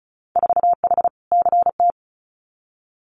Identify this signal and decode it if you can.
Morse code, heard as a keyed tone, about 36 wpm, 700 Hz. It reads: 4H CT
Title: Winter Field Day Exchanges Encore 35wpm